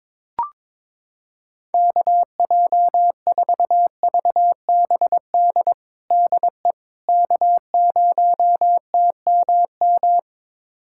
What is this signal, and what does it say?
Morse code, heard as a keyed tone, about 22 wpm, 700 Hz. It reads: KJ4VBD DE K0TMM
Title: USA Callsign DE Callsign 22wpm